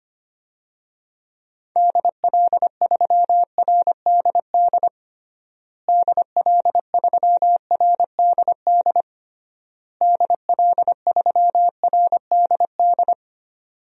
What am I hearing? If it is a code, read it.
Morse code, heard as a keyed tone, about 25 wpm, 700 Hz. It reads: DL3RDD DL3RDD DL3RDD
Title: DXCC-100 Call Signs - 6 Characters - Repeated 3x - 25wpm